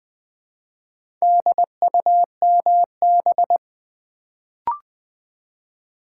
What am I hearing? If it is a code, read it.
Morse code, heard as a keyed tone, about 20 wpm, 700 Hz. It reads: DUMB
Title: Build A Word - 1 Letter At A Time - 3 to 5-Letter Words - 20wpm